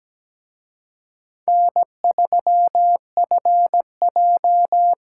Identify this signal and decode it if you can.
Morse code, heard as a keyed tone, about 20 wpm, 700 Hz. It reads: N3FJ
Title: Speed Racing - USA 1x2 Call Signs 17wpm